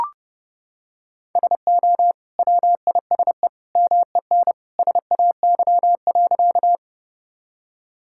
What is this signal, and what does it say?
Morse code, heard as a keyed tone, about 30 wpm, 700 Hz. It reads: SO WISE MEN SAY.
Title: Sentences from Top 1600 Words 30wpm